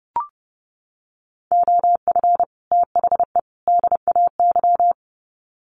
Morse code, heard as a keyed tone, about 30 wpm, 700 Hz. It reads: OF THE DAY
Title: Sets of 3 Words 30wpm